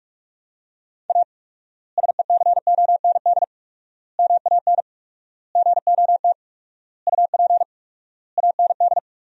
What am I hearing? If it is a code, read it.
Morse code, heard as a keyed tone, about 43 wpm, 700 Hz. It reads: A SECOND MAN GOT UP AND
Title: 4x Word-Spacing - Sentences from Top 300 Words - 45wpm